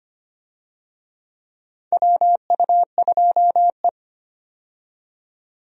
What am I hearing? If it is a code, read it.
Morse code, heard as a keyed tone, about 25 wpm, 700 Hz. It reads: WU2E
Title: USA Call Signs - 4 Characters - 25wpm